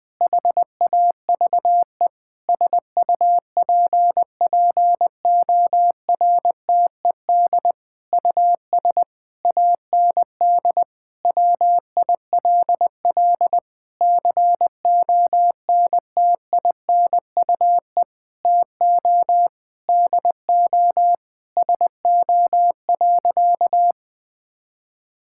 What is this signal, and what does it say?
Morse code, heard as a keyed tone, about 20 wpm, 700 Hz. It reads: HAVE SUPPORTED US AND WILL CONTINUE TO DO SO.
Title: Sentences with 6-Word Phrases - Spoken Beginning - 20wpm